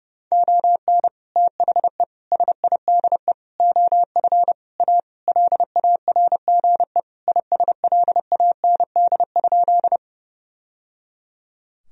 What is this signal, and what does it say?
Morse code, heard as a keyed tone, about 30 wpm, 700 Hz. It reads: ON THE SIDE OF A LARGE ISLAND?
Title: Sentences from Top 1000 Words - Review Part 3 - 30wpm